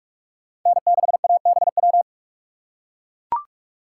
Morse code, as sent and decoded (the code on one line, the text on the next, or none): -. -.... .- -... .--
N6ABW